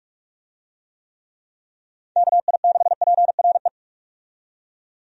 Morse code, heard as a keyed tone, about 44 wpm, 700 Hz. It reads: KI6PRE
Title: USA Call Signs - 6 Characters - 45wpm